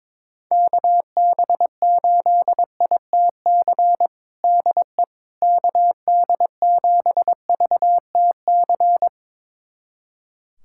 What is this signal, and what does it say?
Morse code, heard as a keyed tone, about 22 wpm, 700 Hz. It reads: KB8ITC DE KD7VTC
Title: USA Callsign DE Callsign - 6 Characters - 22wpm